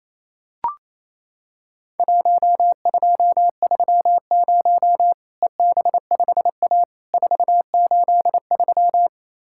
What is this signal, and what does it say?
Morse code, heard as a keyed tone, about 28 wpm, 700 Hz. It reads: 1230 EB5A 483
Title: Worked All Europe DX Contest - QTC Exchange - 28wpm